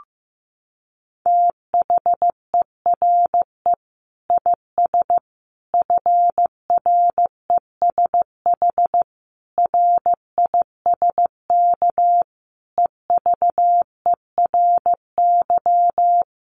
Morse code, heard as a keyed tone, about 15 wpm, 700 Hz. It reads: THERE IS FRESH RISK EVERY
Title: Sentences from Top 1200 Words 15wpm